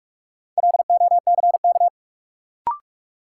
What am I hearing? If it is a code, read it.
Morse code, heard as a keyed tone, about 46 wpm, 700 Hz. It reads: LOCK
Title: Sentences from Top 1300 Words - New Word Drill Alt - 45wpm